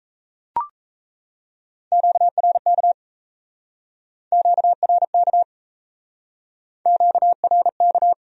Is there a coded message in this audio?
Yes: Morse code from 1.9 to 8.1 s, heard as a keyed tone, about 38 wpm, 700 Hz. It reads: QRK QRK QRK